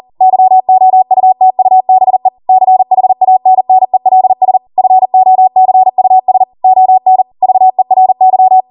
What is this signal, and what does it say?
Morse code, heard as a keyed tone, about 40 wpm, 750 Hz. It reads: YOUTUBE CHANNELS FOCUS ON VERY